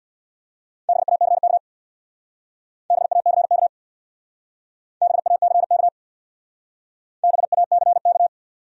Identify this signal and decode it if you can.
Morse code, heard as a keyed tone, about 64 wpm, 700 Hz. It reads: BACK BACK BACK BACK